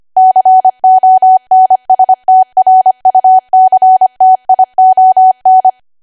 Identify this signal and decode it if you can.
Morse code, heard as a keyed tone, about 25 wpm, 750 Hz. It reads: CONSTRUCTION
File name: Archive / 25 WPM / 25WPM